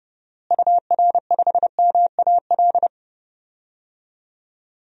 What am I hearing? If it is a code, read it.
Morse code, heard as a keyed tone, about 30 wpm, 700 Hz. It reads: UR5MAL
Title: DXCC-100 Call Signs - 6 Characters - Repeated 3x - 30wpm